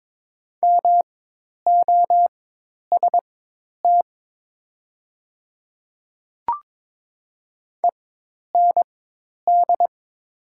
Morse code, heard as a keyed tone, about 22 wpm, spaced as for 10 wpm, 700 Hz. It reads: MOST END
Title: 4x Character Spacing - Top 300 Words - 22wpm